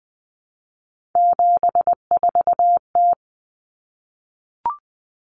Morse code, heard as a keyed tone, about 20 wpm, 700 Hz. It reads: 74T
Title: Common Cut Numbers - 3-Digit 20wpm